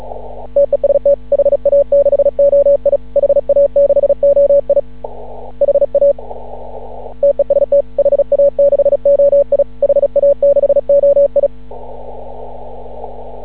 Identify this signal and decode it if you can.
Morse code, heard as a keyed tone, about 36 wpm, 560 Hz. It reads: TEST HA6OI HA6OI HA TEST HA6OI HA6OI
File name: CW / ha6oi 2